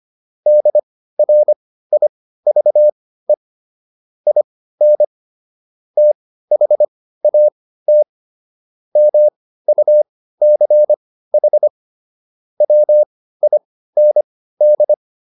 Morse code, heard as a keyed tone, about 25 wpm, spaced as for 15 wpm, 600 Hz. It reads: DRIVE IN THAT MUCH WIND